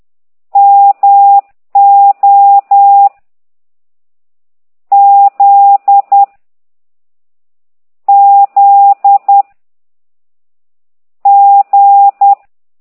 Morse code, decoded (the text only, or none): M O Z Z G